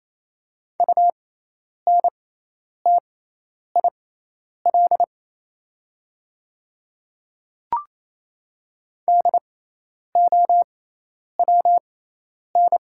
Morse code, heard as a keyed tone, about 28 wpm, spaced as for 10 wpm, 700 Hz. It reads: UNTIL DOWN
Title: 6x Character Spacing - Top 500 Words - 28wpm